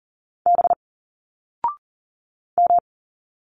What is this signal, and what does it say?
Morse code, heard as a keyed tone, about 39 wpm, 700 Hz. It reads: B M
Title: Single Letters 40wpm